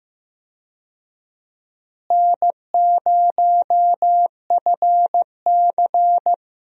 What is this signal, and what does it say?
Morse code, heard as a keyed tone, about 15 wpm, 700 Hz. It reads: N0FC
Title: USA Call Signs - 4 Characters - Repeated 2x - 15wpm